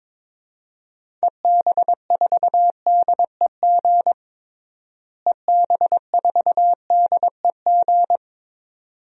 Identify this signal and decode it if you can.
Morse code, heard as a keyed tone, about 22 wpm, 700 Hz. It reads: EB4DEG EB4DEG
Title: DXCC-100 Call Signs - 6 Characters - Repeated 2x - 22wpm